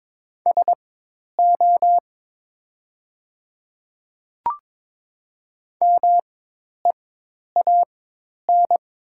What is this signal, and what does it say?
Morse code, heard as a keyed tone, about 22 wpm, spaced as for 10 wpm, 700 Hz. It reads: SO MEAN
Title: 4x Character Spacing - Top 200 Words - 22wpm